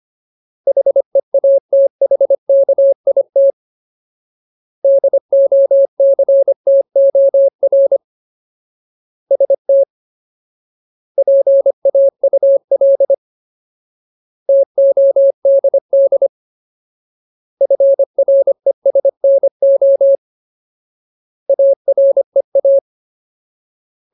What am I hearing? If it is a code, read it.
Morse code, heard as a keyed tone, about 25 wpm, 550 Hz. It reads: HEATHKIT DOCTOR ST PAUL TODD FRESNO AREA
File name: Ultimate Word List / 25WPM Ultimate